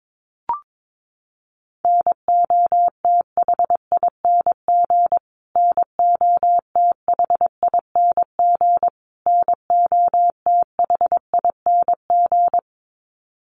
Morse code, heard as a keyed tone, about 22 wpm, 700 Hz. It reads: NOTHING NOTHING NOTHING
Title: Top 500 Words Repeated 22wpm